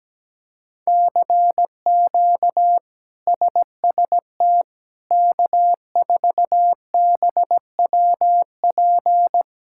Morse code, heard as a keyed tone, about 17 wpm, 700 Hz. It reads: CQ SST K4BWP
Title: K1USN SST QSO Practice - 17wpm